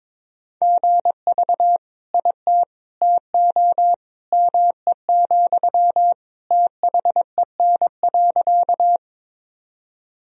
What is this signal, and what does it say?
Morse code, heard as a keyed tone, about 22 wpm, 700 Hz. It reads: GV IT TO ME, THEN.
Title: Sentences from Top 200 Words and Abbreviations 22wpm